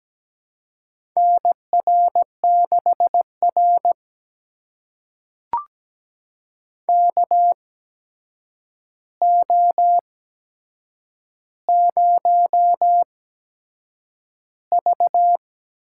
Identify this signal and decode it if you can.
Morse code, heard as a keyed tone, about 17 wpm, spaced as for 7 wpm, 700 Hz. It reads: NR6R KO0V
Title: USA Call Signs - 4 Characters - 8x Character Spacing - 17wpm